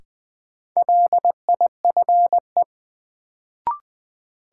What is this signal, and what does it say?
Morse code, heard as a keyed tone, about 20 wpm, 700 Hz. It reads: LIFE
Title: Sentences from Top 300 Words - New Word Drill - 20wpm